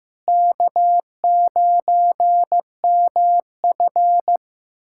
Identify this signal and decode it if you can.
Morse code, heard as a keyed tone, about 15 wpm, 700 Hz. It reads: K9MF
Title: USA Callsign DE Callsign - 4 Characters - 15wpm